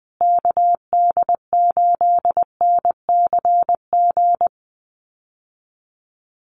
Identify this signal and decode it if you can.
Morse code, heard as a keyed tone, about 20 wpm, 700 Hz. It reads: KD8NCG
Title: USA Call Signs Repeated 3x 20wpm